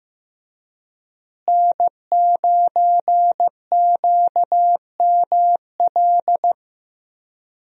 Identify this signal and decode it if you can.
Morse code, heard as a keyed tone, about 15 wpm, 700 Hz. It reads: N9QML
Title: USA Call Signs - 5 Characters - Repeated 2x - 15wpm